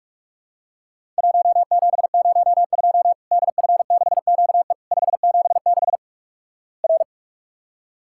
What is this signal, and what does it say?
Morse code, heard as a keyed tone, about 46 wpm, 700 Hz. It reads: 1702 DF6QE 576 R
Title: Worked All Europe DX Contest - QTC Exchange - 45wpm